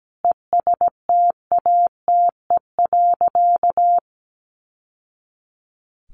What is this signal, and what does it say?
Morse code, heard as a keyed tone, about 17 wpm, 700 Hz. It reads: ESTATE.
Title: Sentences from Top 1200 Words 17wpm